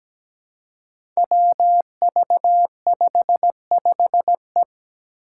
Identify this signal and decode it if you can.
Morse code, heard as a keyed tone, about 17 wpm, 700 Hz. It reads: WV55E
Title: Five Challenging Letter-Number Combinations 17wpm